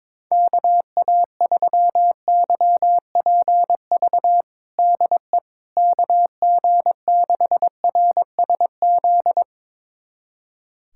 Morse code, heard as a keyed tone, about 22 wpm, 700 Hz. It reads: KA3YPV DE KG6RSZ
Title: USA Callsign DE Callsign 22wpm